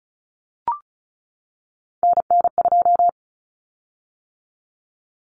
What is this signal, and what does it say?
Morse code, heard as a keyed tone, about 34 wpm, 700 Hz. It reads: NN2
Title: Common Cut Numbers - 3-Digit 35wpm